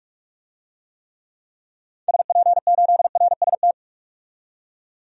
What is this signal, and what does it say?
Morse code, heard as a keyed tone, about 41 wpm, 700 Hz. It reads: SP8RST
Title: DXCC-100 Call Signs - 6 Characters - 45wpm